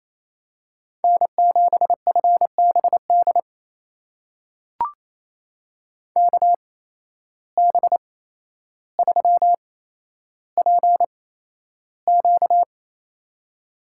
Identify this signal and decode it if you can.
Morse code, heard as a keyed tone, about 28 wpm, spaced as for 12 wpm, 700 Hz. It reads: N7FBD KB3PQ
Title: USA Call Signs - 5 Characters - 8x Character Spacing - 28wpm